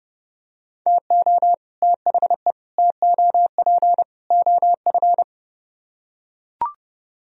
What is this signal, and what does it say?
Morse code, heard as a keyed tone, about 30 wpm, 700 Hz. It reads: TO THE TOP OF